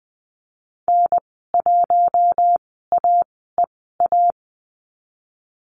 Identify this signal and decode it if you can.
Morse code, heard as a keyed tone, about 20 wpm, spaced as for 14 wpm, 700 Hz. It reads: N1AEA
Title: 088 Lesson 15g - Number 5 - Call Signs 20wpm